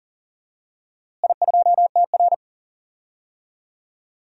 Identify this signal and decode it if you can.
Morse code, heard as a keyed tone, about 40 wpm, 700 Hz. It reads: I2TR
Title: DXCC-100 Call Signs - 4 Characters - Repeated 3x - 40wpm